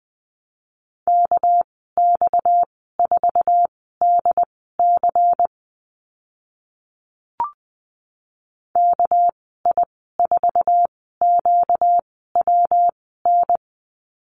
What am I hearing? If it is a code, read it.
Morse code, heard as a keyed tone, about 20 wpm, spaced as for 14 wpm, 700 Hz. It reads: KX4DC KI4QWN
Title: 245 Lesson 39g - Letter z - Call Signs 20wpm